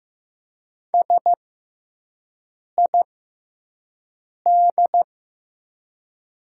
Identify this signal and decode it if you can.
Morse code, heard as a keyed tone, about 15 wpm, spaced as for 5 wpm, 700 Hz. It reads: SID
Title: 6x Character Spacing - Top 200 Words - 15wpm